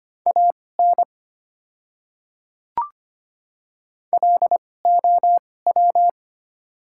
Morse code, heard as a keyed tone, about 25 wpm, spaced as for 18 wpm, 690 Hz. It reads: AN LOW
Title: 2x Character Spacing - Top 200 Words - 25wpm